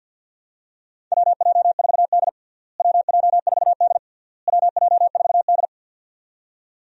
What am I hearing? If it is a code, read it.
Morse code, heard as a keyed tone, about 50 wpm, 700 Hz. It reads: WJ4D WJ4D WJ4D
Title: USA Call Signs - 4 Characters - Repeated 3x - 50wpm